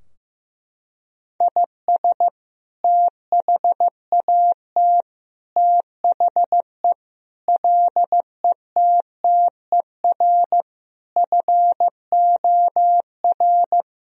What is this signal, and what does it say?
Morse code, heard as a keyed tone, about 15 wpm, 700 Hz. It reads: IS THAT THE LETTER FOR
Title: Sentences from Top 1000 Words - Review Part 2 - 15wpm